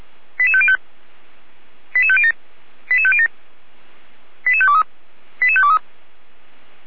Five-tone selective call (ZVEI), heard as quick runs of five tones, tones of about 70 ms. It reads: 88505 88507 88507 88523 88523
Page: SDR/ZVEI